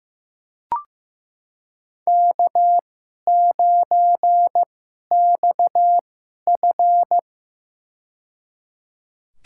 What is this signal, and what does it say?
Morse code, heard as a keyed tone, about 15 wpm, spaced as for 11 wpm, 700 Hz. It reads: K9XF
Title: USA Call Signs - 4 Characters - 2x Character Spacing - 15wpm